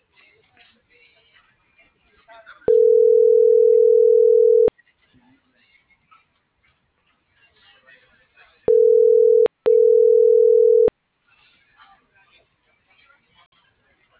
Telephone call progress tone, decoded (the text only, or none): ringback tone